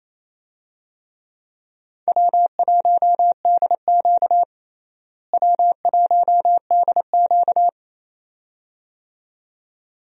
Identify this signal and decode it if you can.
Morse code, heard as a keyed tone, about 28 wpm, 700 Hz. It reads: W1DQ W1DQ
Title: USA Call Signs - 4 Characters - Repeated 2x - 28wpm